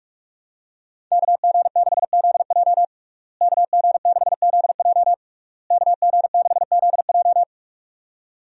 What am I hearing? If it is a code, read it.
Morse code, heard as a keyed tone, about 46 wpm, 700 Hz. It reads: KG6ZJ KG6ZJ KG6ZJ
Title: USA Call Signs - 5 Characters - Repeated 3x - 45wpm